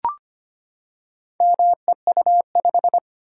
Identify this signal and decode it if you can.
Morse code, heard as a keyed tone, about 25 wpm, 700 Hz. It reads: MEU5
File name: practice / 25wpm / Four Letter-Number Combinations - 25wpm